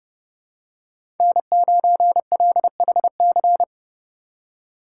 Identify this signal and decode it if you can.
Morse code, heard as a keyed tone, about 30 wpm, 700 Hz. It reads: N9LHC
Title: USA Call Signs - 5 Characters - 4x Character Spacing - 30wpm